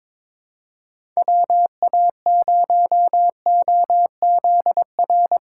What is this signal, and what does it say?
Morse code, heard as a keyed tone, about 22 wpm, 700 Hz. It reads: WA0OZR
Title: USA Call Signs - 6 Characters - 2x Character Spacing - 22wpm